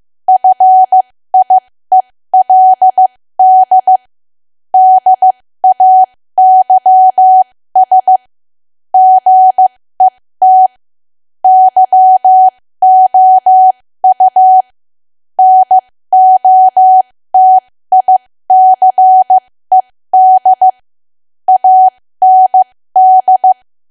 Morse code, heard as a keyed tone, about 15 wpm, spaced as for 13 wpm, 750 Hz. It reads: FIELD DAYS GET YOU NOTICED AND
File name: cwa / W1AW-Level2 / 13WPM